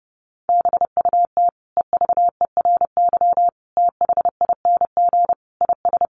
Morse code, heard as a keyed tone, about 30 wpm, 700 Hz. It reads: BUT EVERYTHING IS
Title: Sentences from Top 1100 Words 30wpm